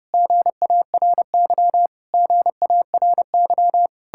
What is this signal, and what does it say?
Morse code, heard as a keyed tone, about 34 wpm, 700 Hz. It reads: GARY GARY